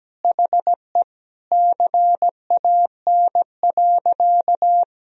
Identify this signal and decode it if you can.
Morse code, heard as a keyed tone, about 17 wpm, 700 Hz. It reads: HE CAN.
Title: Sentences from Top 1000 Words - Review Part 1 - 17wpm